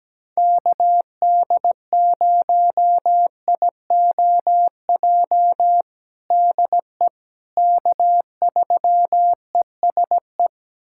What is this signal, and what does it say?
Morse code, heard as a keyed tone, about 17 wpm, 700 Hz. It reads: KD0IOJ DE K3ESE
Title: USA Callsign DE Callsign 17wpm